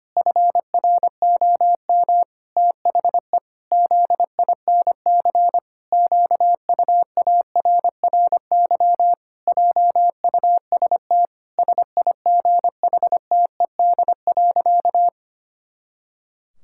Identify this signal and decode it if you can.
Morse code, heard as a keyed tone, about 25 wpm, 700 Hz. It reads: FROM THE ZINC QUARRY JUST SIGHTED.
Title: Pangram Challenge v2 - 25wpm